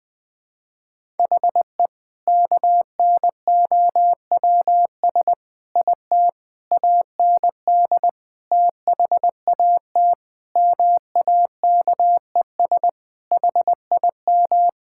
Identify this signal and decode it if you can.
Morse code, heard as a keyed tone, about 20 wpm, 700 Hz. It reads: HE KNOWS IT AND THAT MAKES HIM
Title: Sentences with 2-Word Phrases - Spoken Beginning - 20wpm